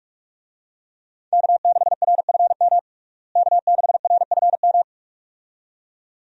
Morse code, heard as a keyed tone, about 46 wpm, 700 Hz. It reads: K6RFM K6RFM
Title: USA Call Signs - 5 Characters - Repeated 2x - 45wpm